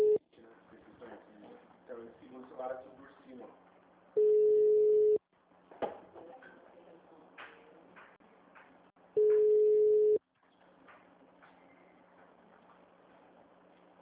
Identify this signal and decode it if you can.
Ringback tone (Europe, 425 Hz), with speech.